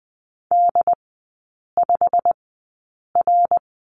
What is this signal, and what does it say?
Morse code, heard as a keyed tone, about 20 wpm, spaced as for 8 wpm, 700 Hz. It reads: D5R
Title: 121 Lesson 20e - Number 3 - Three Characters 20wpm